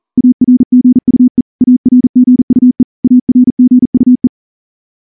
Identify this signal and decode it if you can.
Morse code, heard as a keyed tone, about 38 wpm, 250 Hz. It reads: ARGUE ARGUE ARGUE
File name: morse-training-40WPM